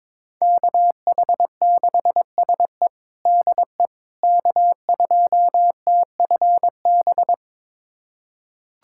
Morse code, heard as a keyed tone, about 22 wpm, 700 Hz. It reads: KH6SE DE K2TFB
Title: USA Callsign DE Callsign - 5 Characters - 22wpm